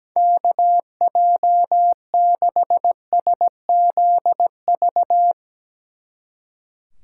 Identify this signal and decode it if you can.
Morse code, heard as a keyed tone, about 17 wpm, 700 Hz. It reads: KJ6SZV
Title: Speed Racing - USA 2x3 Call Signs 17wpm